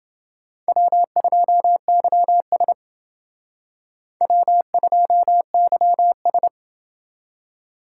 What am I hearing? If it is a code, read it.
Morse code, heard as a keyed tone, about 28 wpm, 700 Hz. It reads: W2YS W2YS